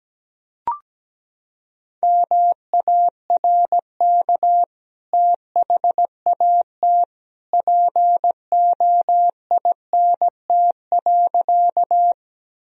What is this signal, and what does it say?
Morse code, heard as a keyed tone, about 17 wpm, 700 Hz. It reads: MARK THAT POINT.
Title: Sentences from Top 1000 Words - Review Part 1 - 17wpm